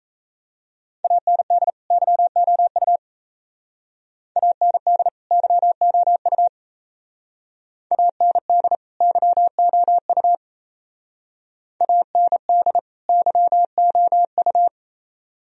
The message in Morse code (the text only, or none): AND YOU AND YOU AND YOU AND YOU